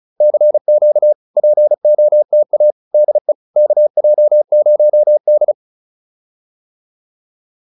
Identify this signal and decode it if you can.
Morse code, heard as a keyed tone, about 37 wpm, 600 Hz. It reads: CQ POTA DE KJ0D